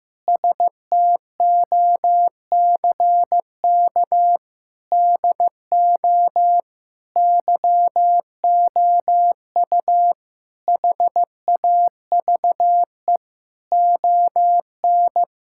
Morse code, heard as a keyed tone, about 15 wpm, 700 Hz. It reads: STOCK DO YOU HAVE ON
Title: Sentences from Top 1000 Words - Review Part 5 - 15wpm